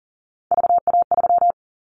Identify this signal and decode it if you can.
Morse code, heard as a keyed tone, about 42 wpm, 690 Hz. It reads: VA3
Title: Callsign Prefixes for Top 100 DXCC - 40wpm